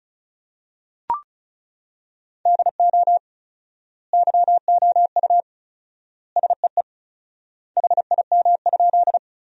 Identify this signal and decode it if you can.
Morse code, heard as a keyed tone, about 34 wpm, 700 Hz. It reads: DO YOU SEE HIM?
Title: 4x Word-Spacing - Sentences from Top 100 Words - 35wpm